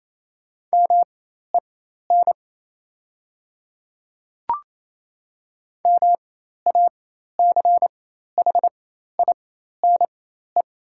Morse code, heard as a keyed tone, about 28 wpm, spaced as for 13 wpm, 700 Hz. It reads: MEN MACHINE
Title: 4x Character Spacing - Top 500 Words - 28wpm